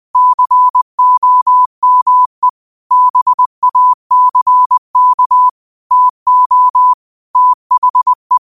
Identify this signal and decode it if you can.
Morse code, heard as a keyed tone, about 20 wpm, 1000 Hz. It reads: COME BACK TO THE
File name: qotdfiles / 20wpm